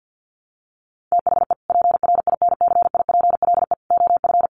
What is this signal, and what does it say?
Morse code, heard as a keyed tone, about 50 wpm, 700 Hz. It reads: THE PRINCIPLE OF
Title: Sentences from Top 1400 Words 50wpm